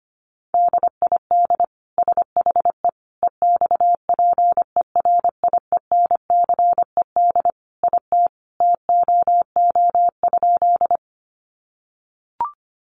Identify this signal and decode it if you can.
Morse code, heard as a keyed tone, about 25 wpm, 700 Hz. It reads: DID SHE EXPERIENCED IT TOO?
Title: Sentences from Top 1100 Words 25wpm